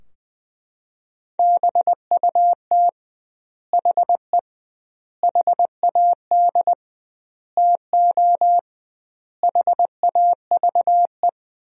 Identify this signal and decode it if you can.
Morse code, heard as a keyed tone, about 21 wpm, 700 Hz. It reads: BUT HE HAD TO HAVE
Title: 2x Word-Spacing - Sentences from Top 100 Words - 20wpm